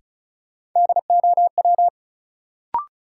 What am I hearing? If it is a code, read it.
Morse code, heard as a keyed tone, about 36 wpm, 700 Hz. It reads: DOW
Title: Sets of 5 Words - New Word Drill - 35wpm